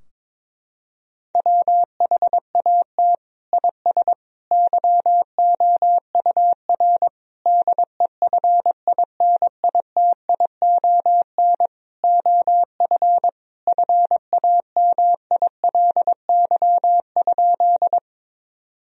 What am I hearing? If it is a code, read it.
Morse code, heard as a keyed tone, about 22 wpm, 700 Hz. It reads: WHAT IS YOUR DEFINITION OF FAMILY?
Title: Sentences from Top 1000 Words - Review Part 4 - 22wpm